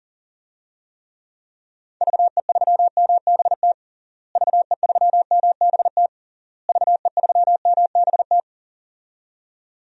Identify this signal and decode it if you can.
Morse code, heard as a keyed tone, about 37 wpm, 700 Hz. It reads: VE3MBT VE3MBT VE3MBT
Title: DXCC-100 Call Signs - 6 Characters - Repeated 3x - 40wpm